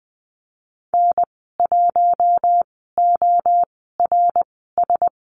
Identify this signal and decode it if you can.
Morse code, heard as a keyed tone, about 20 wpm, spaced as for 14 wpm, 700 Hz. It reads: N1ORS
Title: 069 Lesson 12g - Letter d - Call Signs 20wpm